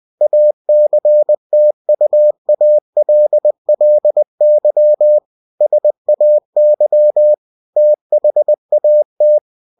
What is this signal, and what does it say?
Morse code, heard as a keyed tone, about 20 wpm, 600 Hz. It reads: ACTUALLY SAY THAT
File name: cw-books / winnie-the-pooh / winnie-20wpm-page-0045